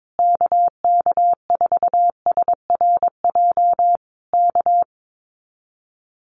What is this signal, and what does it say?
Morse code, heard as a keyed tone, about 22 wpm, 700 Hz. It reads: KK4SRJ K
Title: Common QSO Phrases Encore 22wpm